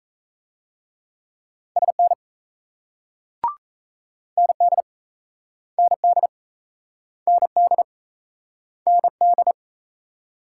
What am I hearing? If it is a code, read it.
Morse code, heard as a keyed tone, about 38 wpm, 700 Hz. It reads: SN ND ND ND ND